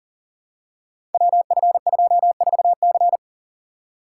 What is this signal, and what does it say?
Morse code, heard as a keyed tone, about 40 wpm, 700 Hz. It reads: WF24C
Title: Code Groups - 5 Letter:Numbers - 40wpm